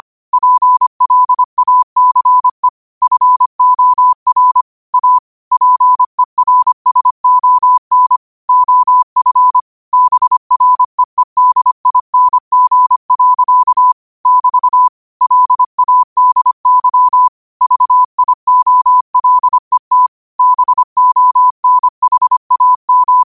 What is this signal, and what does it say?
Morse code, heard as a keyed tone, about 25 wpm, 1000 Hz. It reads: PLACE FOR A PERSON OF BREEDING. <BT> LADY VIOLET BONHAM